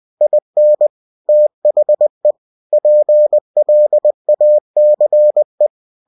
Morse code, heard as a keyed tone, about 20 wpm, 600 Hz. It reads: IN THE PLACE
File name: cw-books / winnie-the-pooh / winnie-20wpm-page-0096